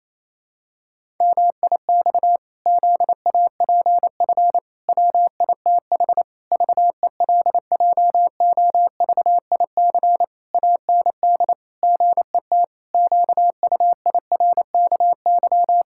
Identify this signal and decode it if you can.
Morse code, heard as a keyed tone, about 27 wpm, 700 Hz. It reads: MIX ZAPF WITH VELJOVIC AND GET QUIRKY
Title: Pangram Challenge v2 - 28wpm